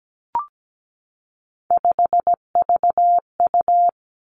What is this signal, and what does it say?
Morse code, heard as a keyed tone, about 17 wpm, 700 Hz. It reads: 5VU
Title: Three Challenging Letter-Number Combinations 17wpm